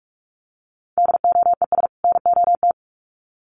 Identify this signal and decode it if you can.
Morse code, heard as a keyed tone, about 44 wpm, 700 Hz. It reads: DOES NOT
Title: Sets of 2 Words 45wpm